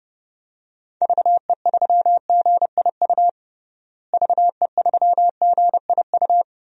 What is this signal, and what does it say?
Morse code, heard as a keyed tone, about 31 wpm, 700 Hz. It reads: VE3GIU VE3GIU
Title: DXCC-100 Call Signs - 6 Characters - Repeated 3x - 30wpm